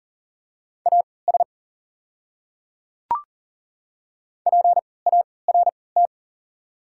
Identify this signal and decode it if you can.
Morse code, heard as a keyed tone, about 40 wpm, spaced as for 23 wpm, 700 Hz. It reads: AS PART
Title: 3x Character Spacing - Top 300 Words - 40wpm